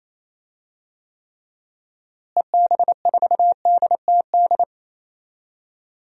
Morse code, heard as a keyed tone, about 28 wpm, 700 Hz. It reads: EB4DTD
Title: DXCC-100 Call Signs - 6 Characters - Repeated 3x - 28wpm